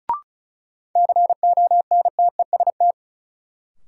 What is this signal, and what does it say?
Morse code, heard as a keyed tone, about 36 wpm, 700 Hz. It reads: CONTEST
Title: Sentences from Top 1600 Words - New Word Drill - 35wpm